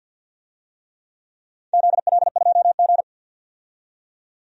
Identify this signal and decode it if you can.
Morse code, heard as a keyed tone, about 53 wpm, 700 Hz. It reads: ZL2G
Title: DXCC-100 Call Signs - 4 Characters - 50wpm